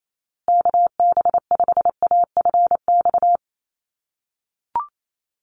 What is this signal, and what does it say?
Morse code, heard as a keyed tone, about 28 wpm, 700 Hz. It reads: KB5AFX
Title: USA Call Signs Repeated 3x 28wpm